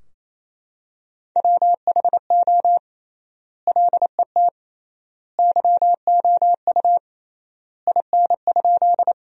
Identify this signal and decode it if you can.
Morse code, heard as a keyed tone, about 27 wpm, 700 Hz. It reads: WHO LET YOU IN?
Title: 3x Word-Spacing - Sentences from Top 300 Words - 28wpm